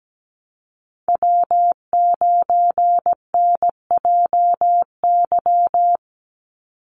Morse code, heard as a keyed tone, about 17 wpm, 700 Hz. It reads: W9NJY
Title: Call Signs - 17wpm